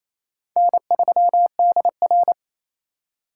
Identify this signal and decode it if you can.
Morse code, heard as a keyed tone, about 34 wpm, 700 Hz. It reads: N3DR
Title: Speed Racing - USA 1x2 Call Signs 28wpm